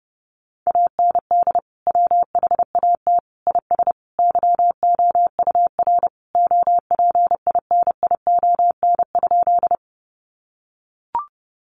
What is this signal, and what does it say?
Morse code, heard as a keyed tone, about 30 wpm, 700 Hz. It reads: AND WHAT IS YOUR OPINION?
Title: Sentences from Top 1200 Words 30wpm